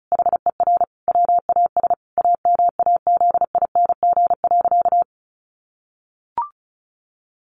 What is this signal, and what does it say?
Morse code, heard as a keyed tone, about 34 wpm, 700 Hz. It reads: HER WAS AMAZING.
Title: Sentences from Top 1500 Words 35wpm